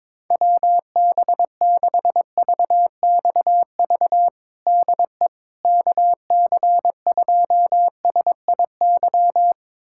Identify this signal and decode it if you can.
Morse code, heard as a keyed tone, about 22 wpm, 700 Hz. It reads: WB6VXV DE KC2SIY
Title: USA Callsign DE Callsign 22wpm